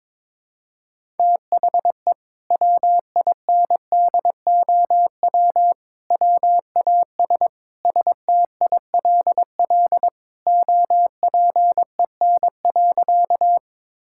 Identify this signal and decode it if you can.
Morse code, heard as a keyed tone, about 22 wpm, 700 Hz. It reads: THE WINDOW WAS STILL OPEN.
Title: Sentences from Top 1600 Words 22wpm